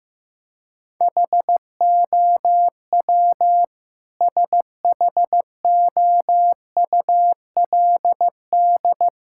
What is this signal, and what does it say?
Morse code, heard as a keyed tone, about 15 wpm, 700 Hz. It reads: HOW SHOULD
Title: Sentences from Top 1000 Words - Review Part 5 - 15wpm